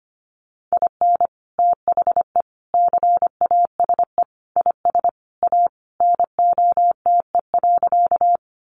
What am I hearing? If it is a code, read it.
Morse code, heard as a keyed tone, about 25 wpm, 700 Hz. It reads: IN THE CASE IS A NOTE.
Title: Sentences from Top 700 Words 25wpm